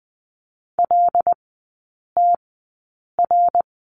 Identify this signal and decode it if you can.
Morse code, heard as a keyed tone, about 20 wpm, spaced as for 8 wpm, 700 Hz. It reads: LTR
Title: 074 Lesson 13e - Letter L - Three Characters 20wpm